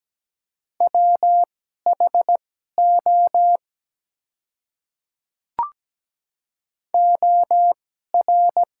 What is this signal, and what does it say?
Morse code, heard as a keyed tone, about 17 wpm, spaced as for 12 wpm, 700 Hz. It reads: WHO OR
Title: 2x Character Spacing - Top 100 Words - 17wpm